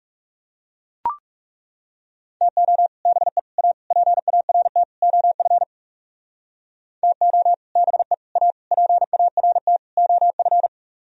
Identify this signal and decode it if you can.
Morse code, heard as a keyed tone, about 37 wpm, 700 Hz. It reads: TO BE A PART OF TO BE A PART OF
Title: Speed-Racing - Sets of 5 Words 30wpm